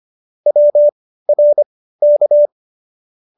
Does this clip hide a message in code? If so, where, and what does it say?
Yes: Morse code from 0.5 to 2.5 s, heard as a keyed tone, about 25 wpm, spaced as for 15 wpm, 590 Hz. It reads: WRK